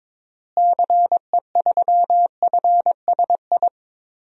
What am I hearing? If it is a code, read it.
Morse code, heard as a keyed tone, about 22 wpm, 700 Hz. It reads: CE3FSI
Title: DXCC-100 Call Signs - 6 Characters - Repeated 2x - 22wpm